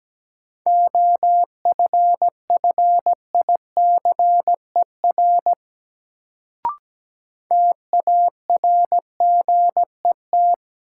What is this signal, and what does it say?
Morse code, heard as a keyed tone, about 17 wpm, 700 Hz. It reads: OFFICER TARGET
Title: Sentences from Top 1200 Words - New Word Drill - 17wpm